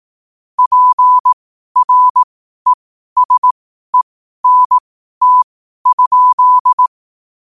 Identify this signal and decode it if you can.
Morse code, heard as a keyed tone, about 18 wpm, spaced as for 12 wpm, 1000 Hz. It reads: PRESENT?
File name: qotdfiles / 12wpm